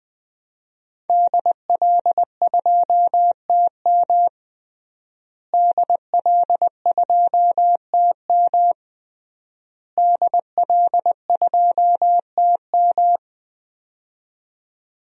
Morse code, heard as a keyed tone, about 20 wpm, 700 Hz. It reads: DL2TM DL2TM DL2TM
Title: DXCC-100 Call Signs - 5 Characters - Repeated 3x - 20wpm